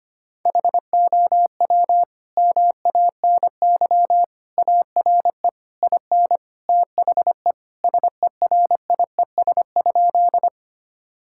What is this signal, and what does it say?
Morse code, heard as a keyed tone, about 25 wpm, 700 Hz. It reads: HOW MANY ARE IN THE SERIES?
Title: Sentences from Top 1000 Words - Review Part 5 - 25wpm